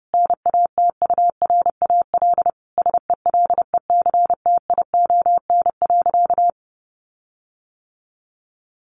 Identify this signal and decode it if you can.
Morse code, heard as a keyed tone, about 30 wpm, 700 Hz. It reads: NATURAL SELECTION.
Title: Sentences from Top 1400 Words 30wpm